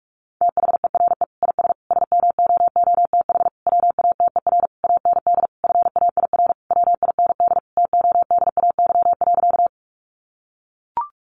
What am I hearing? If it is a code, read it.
Morse code, heard as a keyed tone, about 44 wpm, 700 Hz. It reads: THERE IS SMOOTH WATER AND FAIR WIND TODAY.
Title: Sentences from Top 1400 Words 45wpm